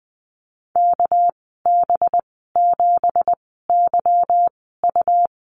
Morse code, heard as a keyed tone, about 20 wpm, spaced as for 15 wpm, 700 Hz. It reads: KB7YU
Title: 231 Lesson 37g - Letter x - Call Signs 20wpm